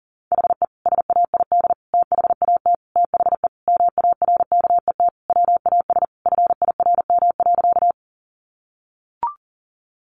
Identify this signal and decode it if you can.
Morse code, heard as a keyed tone, about 40 wpm, 700 Hz. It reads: HE SAID THAT THE MARKET WAS FIRM.
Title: Sentences from Top 1400 Words 40wpm